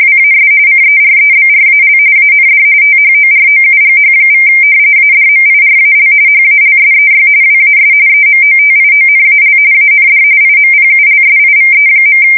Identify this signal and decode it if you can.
RTTY, about 45.45 baud. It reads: WAS CEREBRAL AND THOROUGH= ADMIRAL HALSEY WAS EMOTIONAL AND IMPULSIVE.
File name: fsk170 / battleshipnj